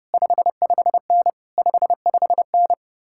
Morse code, heard as a keyed tone, about 33 wpm, 700 Hz. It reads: 55N 55N